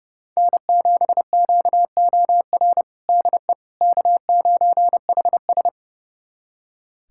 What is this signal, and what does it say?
Morse code, heard as a keyed tone, about 30 wpm, 700 Hz. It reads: N7QOR DE K9HS
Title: USA Callsign DE Callsign 30wpm